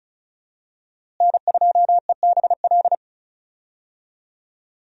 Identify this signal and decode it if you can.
Morse code, heard as a keyed tone, about 34 wpm, 700 Hz. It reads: N2EBL
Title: USA Call Signs - 5 Characters - Repeated 2x - 35wpm